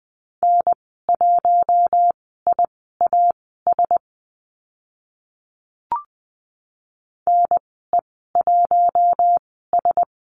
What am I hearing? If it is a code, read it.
Morse code, heard as a keyed tone, about 20 wpm, spaced as for 14 wpm, 700 Hz. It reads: N1IAS NE1S
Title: 042 Lesson 8g - Number 1 - Call Signs 20wpm